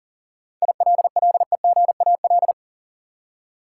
Morse code, heard as a keyed tone, about 40 wpm, 700 Hz. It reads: ILLEGAL
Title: Seven Letter Words - 40wpm